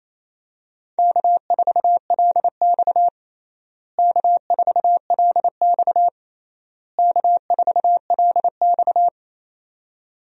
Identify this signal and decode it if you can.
Morse code, heard as a keyed tone, about 28 wpm, 700 Hz. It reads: K4LX K4LX K4LX
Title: USA Call Signs - 4 Characters - Repeated 3x - 28wpm